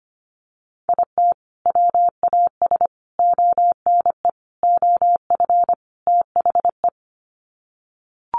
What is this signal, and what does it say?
Morse code, heard as a keyed tone, about 25 wpm, 700 Hz. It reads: IT WAS ONE OF THE
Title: Sets of 5 Words 25wpm